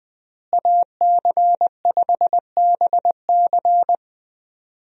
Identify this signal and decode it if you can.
Morse code, heard as a keyed tone, about 20 wpm, 700 Hz. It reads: AC5BC
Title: USA Call Signs - 5 Characters - Repeated 2x - 20wpm